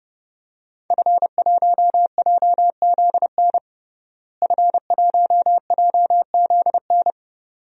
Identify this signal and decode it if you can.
Morse code, heard as a keyed tone, about 32 wpm, 700 Hz. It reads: F1JZN F1JZN
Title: DXCC-100 Call Signs - 5 Characters - Repeated 2x - 30wpm